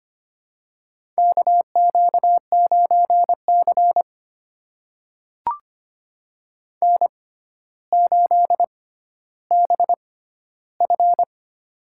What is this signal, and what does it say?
Morse code, heard as a keyed tone, about 25 wpm, spaced as for 13 wpm, 700 Hz. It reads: KQ9C N8BF
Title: USA Call Signs - 4 Characters - 6x Character Spacing - 25wpm